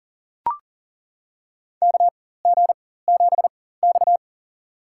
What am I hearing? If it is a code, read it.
Morse code, heard as a keyed tone, about 40 wpm, spaced as for 19 wpm, 700 Hz. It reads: KG7X
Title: USA Call Signs - 4 Characters - 4x Character Spacing - 40wpm